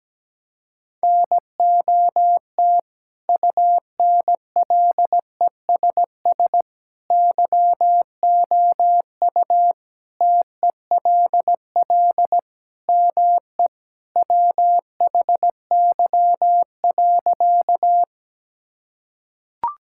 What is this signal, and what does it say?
Morse code, heard as a keyed tone, about 17 wpm, 700 Hz. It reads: NOT UNLESS YOU TELL ME WHY.
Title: Sentences from Top 1000 Words - Review Part 6 - 17wpm